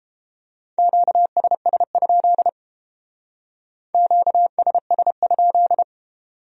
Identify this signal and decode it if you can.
Morse code, heard as a keyed tone, about 30 wpm, 700 Hz. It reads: QSS? QSS?